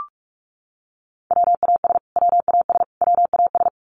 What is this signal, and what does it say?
Morse code, heard as a keyed tone, about 44 wpm, 700 Hz. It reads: WAS WAS WAS
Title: Top 500 Words Repeated 45wpm